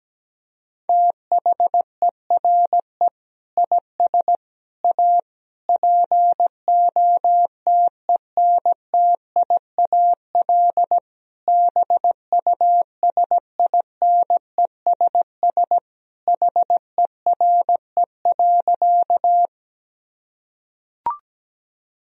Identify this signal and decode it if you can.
Morse code, heard as a keyed tone, about 17 wpm, 700 Hz. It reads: THERE IS A POTENTIAL BUSINESS HERE.
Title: Sentences from Top 1000 Words - Review Part 4 - 17wpm